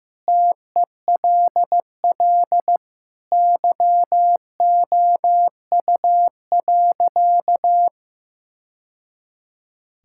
Morse code, heard as a keyed tone, about 15 wpm, 700 Hz. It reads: TELL YOU.
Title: Sentences from Top 1000 Words - Review Part 5 - 15wpm